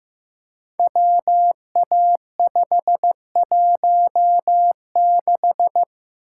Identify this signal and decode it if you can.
Morse code, heard as a keyed tone, about 15 wpm, 700 Hz. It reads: WA516
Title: Code Groups - 5 Letter:Numbers - 15wpm